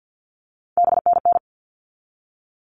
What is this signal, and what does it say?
Morse code, heard as a keyed tone, about 50 wpm, 700 Hz. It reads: BNN
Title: All Cut Numbers - 3-Digit 50wpm